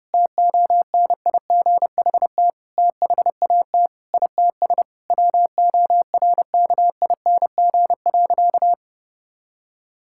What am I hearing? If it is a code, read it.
Morse code, heard as a keyed tone, about 32 wpm, 700 Hz. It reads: TONIGHT THAT ITS WORKING.
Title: Sentences from Top 1000 Words - Review Part 6 - 30wpm